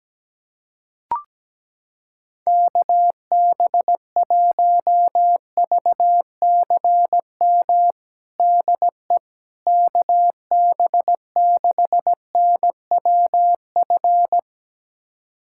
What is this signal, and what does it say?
Morse code, heard as a keyed tone, about 17 wpm, 700 Hz. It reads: KB1VCM DE KB6NWF
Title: USA Callsign DE Callsign - 6 Characters - 17wpm